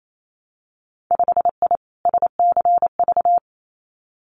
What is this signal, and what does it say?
Morse code, heard as a keyed tone, about 28 wpm, 700 Hz. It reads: 5I SCV
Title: Winter Field Day Exchanges Encore 28wpm